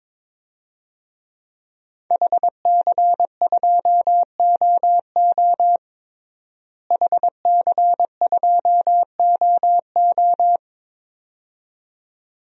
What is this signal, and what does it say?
Morse code, heard as a keyed tone, about 22 wpm, 700 Hz. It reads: HC2OO HC2OO
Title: DXCC-100 Call Signs - 5 Characters - Repeated 2x - 22wpm